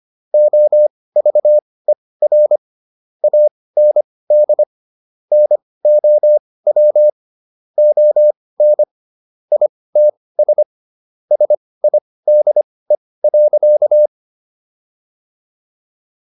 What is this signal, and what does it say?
Morse code, heard as a keyed tone, about 25 wpm, spaced as for 18 wpm, 600 Hz. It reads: OVER AND NOW ON ITS SIDE.